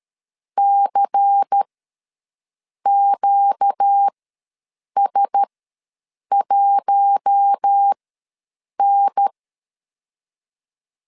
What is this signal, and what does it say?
Morse code, heard as a keyed tone, about 13 wpm, spaced as for 7 wpm, 800 Hz. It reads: CQS1N
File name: LEZIONI / 5 LBVC / 5 LBVC ws 12wpm 3159